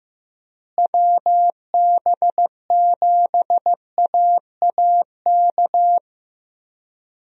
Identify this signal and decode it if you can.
Morse code, heard as a keyed tone, about 15 wpm, 700 Hz. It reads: WB7AAK
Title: USA Call Signs - 6 Characters - Repeated 2x - 15wpm